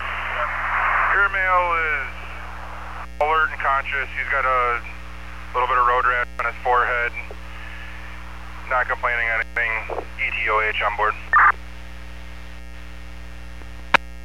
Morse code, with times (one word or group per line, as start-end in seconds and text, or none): none